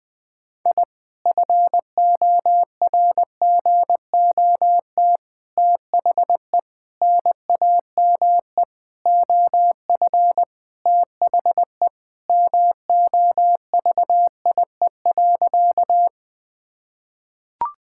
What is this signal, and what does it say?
Morse code, heard as a keyed tone, about 20 wpm, 690 Hz. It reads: I FORGOT THE NAME OF THE MOVIE.
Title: Sentences with 4-Word Phrases - Spoken Beginning - 20wpm